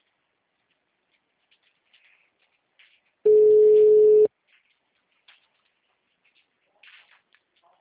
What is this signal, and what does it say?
Ringback tone (Europe, 425 Hz).